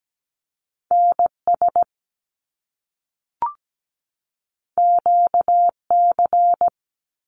Morse code, heard as a keyed tone, about 17 wpm, 700 Hz. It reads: NS QC
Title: Canadian Provinces and Territories 17wpm